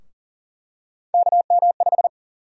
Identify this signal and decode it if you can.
Morse code, heard as a keyed tone, about 40 wpm, 700 Hz. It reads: KM5
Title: USA Call Sign Prefixes - 40wpm